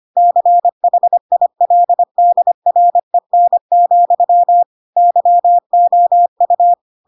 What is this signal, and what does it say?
Morse code, heard as a keyed tone, about 25 wpm, 700 Hz. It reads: CHILDREN, YOU
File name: Pinocchio / Pinocchio ch1 25 25 ik1hge